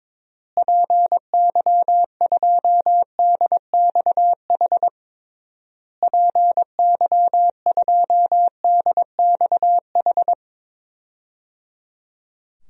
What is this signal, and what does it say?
Morse code, heard as a keyed tone, about 22 wpm, 700 Hz. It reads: PY2DXH PY2DXH
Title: DXCC-100 Call Signs - 6 Characters - Repeated 3x - 22wpm